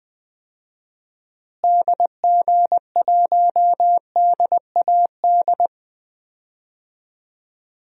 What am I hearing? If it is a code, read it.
Morse code, heard as a keyed tone, about 20 wpm, 700 Hz. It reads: DG1DAD
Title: DXCC-100 Call Signs - 6 Characters - Repeated 2x - 20wpm